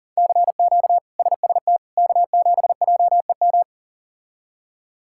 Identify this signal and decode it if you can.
Morse code, heard as a keyed tone, about 40 wpm, 700 Hz. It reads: CQ SST K7JEM
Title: K1USN SST QSO Practice - 40wpm